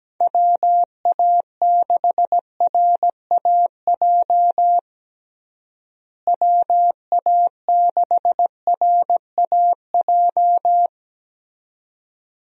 Morse code, heard as a keyed tone, about 17 wpm, 700 Hz. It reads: WA6RAJ WA6RAJ
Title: USA Call Signs - 6 Characters - Repeated 2x - 17wpm